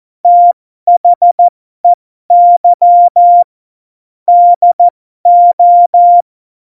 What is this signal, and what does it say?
Morse code, heard as a keyed tone, about 14 wpm, spaced as for 12 wpm, 700 Hz. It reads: THEY DO
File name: Morsekurs / Witze